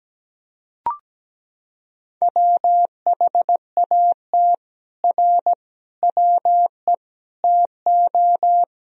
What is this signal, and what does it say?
Morse code, heard as a keyed tone, about 17 wpm, 700 Hz. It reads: WHAT R WE TO
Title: Sentences from Top 100 Words and Abbreviations 17wpm